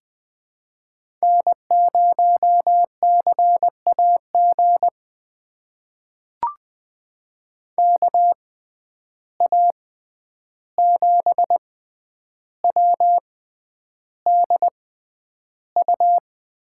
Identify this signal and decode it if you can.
Morse code, heard as a keyed tone, about 20 wpm, spaced as for 10 wpm, 700 Hz. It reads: N0CAG KA7WDU
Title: USA Call Signs - 6x Character Spacing - 20wpm